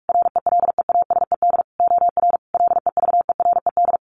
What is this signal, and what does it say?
Morse code, heard as a keyed tone, about 44 wpm, 700 Hz. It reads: RELEASED OR LEVERED
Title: A Sound Of Thunder 45wpm